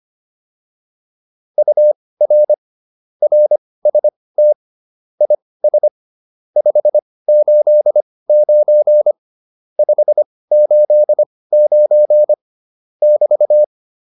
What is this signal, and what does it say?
Morse code, heard as a keyed tone, about 25 wpm, spaced as for 18 wpm, 600 Hz. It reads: UR RST IS 589 589 <BT>
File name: QSO302 18